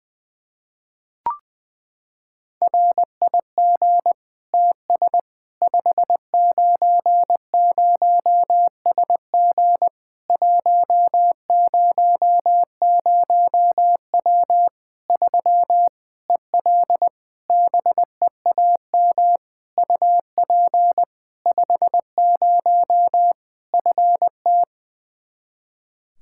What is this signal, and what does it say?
Morse code, heard as a keyed tone, about 21 wpm, 700 Hz. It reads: RIG TS 590SG 100W 3 EL BEAM UP 50 FT
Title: K1USN SST QSO Practice 20wpm